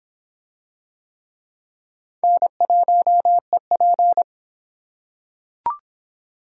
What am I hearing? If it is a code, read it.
Morse code, heard as a keyed tone, about 23 wpm, 700 Hz. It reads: N1EP